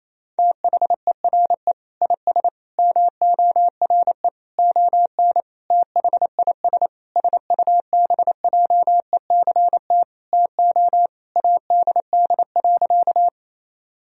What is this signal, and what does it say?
Morse code, heard as a keyed tone, about 28 wpm, 700 Hz. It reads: THERE IS MORE ON THIS SUBJECT TO ADD.
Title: Sentences from Top 1000 Words - Review Part 1 - 28wpm